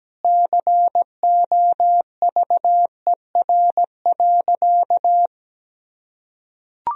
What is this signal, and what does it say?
Morse code, heard as a keyed tone, about 17 wpm, 700 Hz. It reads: COVER.
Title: Sentences from Top 1000 Words - Review Part 1 - 17wpm